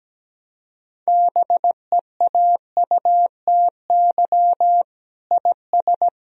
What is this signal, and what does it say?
Morse code, heard as a keyed tone, about 17 wpm, 700 Hz. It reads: BEAUTY IS
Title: Sentences from Top 1000 Words - Review Part 3 - 17wpm